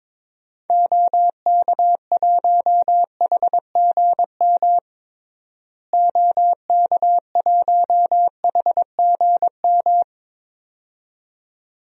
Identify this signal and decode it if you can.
Morse code, heard as a keyed tone, about 22 wpm, 700 Hz. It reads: OK1HGM OK1HGM
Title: DXCC-100 Call Signs - 6 Characters - Repeated 2x - 22wpm